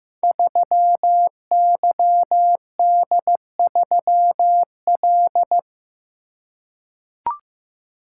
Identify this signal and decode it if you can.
Morse code, heard as a keyed tone, about 15 wpm, 700 Hz. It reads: 3YD3L
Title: Code Groups - 5 Letter:Numbers - 15wpm